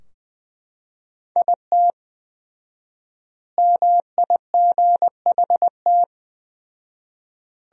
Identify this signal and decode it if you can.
Morse code, heard as a keyed tone, about 20 wpm, 700 Hz. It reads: IT MIGHT
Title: 4x Word-Spacing - Sentences from Top 500 Words - 20wpm